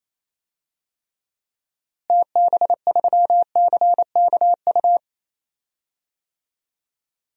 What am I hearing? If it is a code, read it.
Morse code, heard as a keyed tone, about 28 wpm, 700 Hz. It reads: TB3CKU
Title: DXCC-100 Call Signs - 6 Characters - Repeated 2x - 28wpm